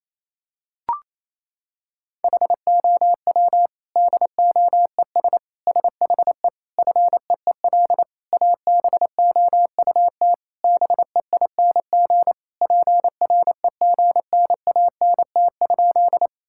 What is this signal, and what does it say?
Morse code, heard as a keyed tone, about 28 wpm, 700 Hz. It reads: HOW DOES SHE FEEL ABOUT BEING PREGNANT?
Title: Sentences from Top 1000 Words - Review Part 6 - 28wpm